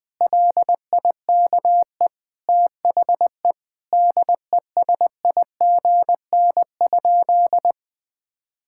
Morse code, heard as a keyed tone, about 20 wpm, 700 Hz. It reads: LIKE THE DESIGN?
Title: Sentences from Top 1000 Words - Review Part 5 - 20wpm